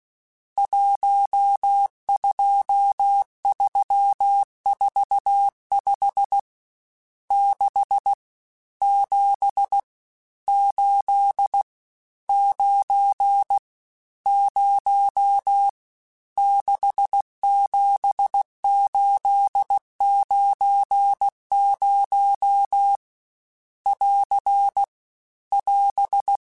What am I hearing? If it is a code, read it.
Morse code, heard as a keyed tone, about 16 wpm, 800 Hz. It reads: 12345 6 7 8 9 0 67890 <AR> <AS>
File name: Misc / Misc-mp3s / Numb @15